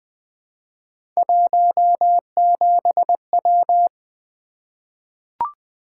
Morse code, heard as a keyed tone, about 20 wpm, 700 Hz. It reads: 17W
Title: Three Letter-Number Combinations - 20wpm